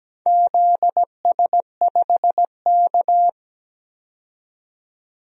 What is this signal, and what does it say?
Morse code, heard as a keyed tone, about 17 wpm, 700 Hz. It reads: ZS5K
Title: DXCC-100 Call Signs - 4 Characters - 17wpm